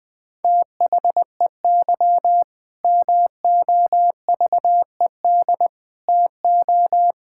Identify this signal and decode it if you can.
Morse code, heard as a keyed tone, about 20 wpm, 700 Hz. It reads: THEY MOVED TO
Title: Sentences with 4-Word Phrases - 20wpm